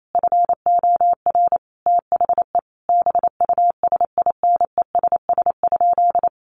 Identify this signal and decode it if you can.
Morse code, heard as a keyed tone, about 28 wpm, 700 Hz. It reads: FOR THE BUSINESS?
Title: Sentences from Top 1400 Words 28wpm